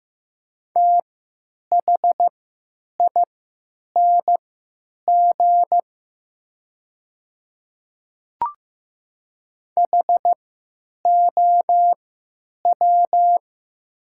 Morse code, heard as a keyed tone, about 15 wpm, spaced as for 9 wpm, 700 Hz. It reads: THING HOW
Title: 3x Character Spacing - Top 300 Words - 15wpm